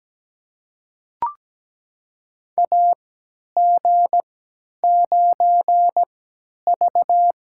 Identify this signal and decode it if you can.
Morse code, heard as a keyed tone, about 17 wpm, spaced as for 10 wpm, 700 Hz. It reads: AG9V
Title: USA Call Signs - 4 Characters - 3x Character Spacing - 17wpm